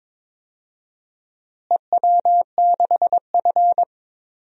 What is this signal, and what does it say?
Morse code, heard as a keyed tone, about 22 wpm, 700 Hz. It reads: EW6F
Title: DXCC-100 Call Signs - 4 Characters - Repeated 2x - 22wpm